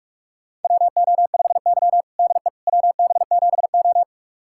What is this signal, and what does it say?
Morse code, heard as a keyed tone, about 46 wpm, 700 Hz. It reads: WO5Y DE WB7O
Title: USA Callsign DE Callsign - 4 Characters - 45wpm